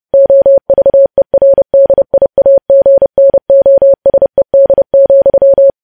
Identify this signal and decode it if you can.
Morse code, heard as a keyed tone, about 30 wpm, 550 Hz. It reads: OVERDIAGNOSED,
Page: News Headlines in Morse Code at 30 WPM